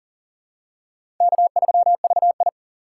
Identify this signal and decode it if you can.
Morse code, heard as a keyed tone, about 40 wpm, 700 Hz. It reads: K3VI
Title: USA Call Signs - 4 Characters - 3x Character Spacing - 40wpm